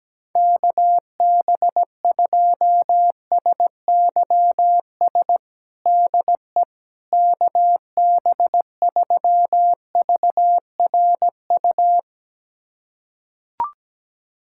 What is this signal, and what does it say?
Morse code, heard as a keyed tone, about 17 wpm, 700 Hz. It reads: KB2SYS DE KB3VRU
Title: USA Callsign DE Callsign - 6 Characters - 17wpm